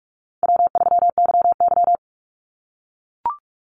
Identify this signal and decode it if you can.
Morse code, heard as a keyed tone, about 44 wpm, 700 Hz. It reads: W3YY
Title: Call Signs - 45wpm